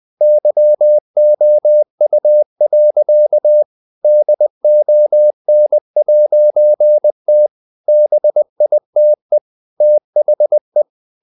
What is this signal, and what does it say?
Morse code, heard as a keyed tone, about 20 wpm, 600 Hz. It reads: YOU. DON'T BITE THE
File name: CW / Practice / EnglishSayings20